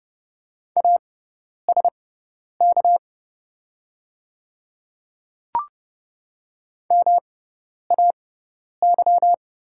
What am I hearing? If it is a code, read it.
Morse code, heard as a keyed tone, about 31 wpm, spaced as for 10 wpm, 700 Hz. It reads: ASK MAY
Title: 6x Character Spacing - Top 200 Words - 30wpm